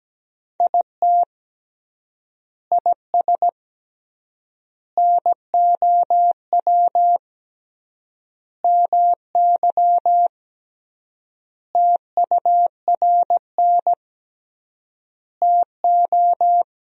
Morse code, heard as a keyed tone, about 17 wpm, 700 Hz. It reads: IT IS NOW MY TURN TO
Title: 3x Word-Spacing - Sentences from Top 200 Words - 17wpm